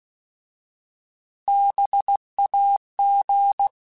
Morse code, heard as a keyed tone, about 16 wpm, 800 Hz. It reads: BAG
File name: Alphabetical / Alphabetical mp3s / Bs @15